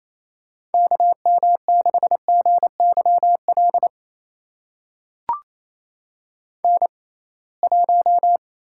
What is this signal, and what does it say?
Morse code, heard as a keyed tone, about 28 wpm, spaced as for 21 wpm, 700 Hz. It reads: KM6GYL N1
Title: USA Call Signs - 6x Character Spacing - 28wpm